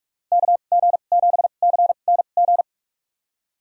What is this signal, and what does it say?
Morse code, heard as a keyed tone, about 46 wpm, spaced as for 32 wpm, 700 Hz. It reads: KG7CNG
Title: USA Call Signs - 6 Characters - 2x Character Spacing - 45wpm